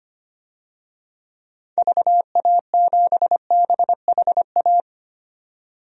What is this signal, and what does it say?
Morse code, heard as a keyed tone, about 25 wpm, 700 Hz. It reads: VA7BHA
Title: DXCC-100 Call Signs - 6 Characters - 25wpm